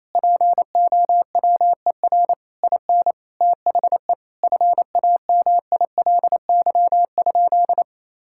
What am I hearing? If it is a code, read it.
Morse code, heard as a keyed tone, about 28 wpm, 700 Hz. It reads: POWER IN THE FAMILY?
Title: Sentences from Top 1000 Words - Review Part 6 - 28wpm